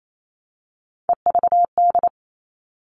Morse code, heard as a keyed tone, about 28 wpm, 700 Hz. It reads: EVD
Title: All Cut Numbers - 3-Digit 28wpm